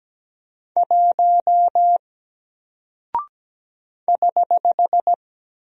Morse code, heard as a keyed tone, about 17 wpm, 700 Hz. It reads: 1 <HH>
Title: Single Character (Letter:Number:Punctuation:Prosign) Alt 17wpm